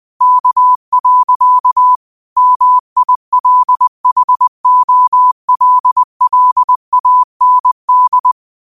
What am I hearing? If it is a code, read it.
Morse code, heard as a keyed tone, about 20 wpm, 1000 Hz. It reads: K. MILHOLLAND
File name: qotdfiles / 20wpm